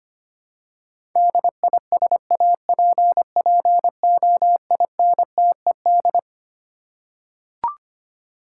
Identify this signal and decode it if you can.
Morse code, heard as a keyed tone, about 25 wpm, 700 Hz. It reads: DISAPPOINTED
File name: Twelve Letter Words - 25wpm